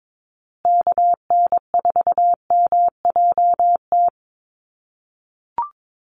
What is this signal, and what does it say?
Morse code, heard as a keyed tone, about 22 wpm, 700 Hz. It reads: KN4MJT
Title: USA Call Signs Repeated 3x 22wpm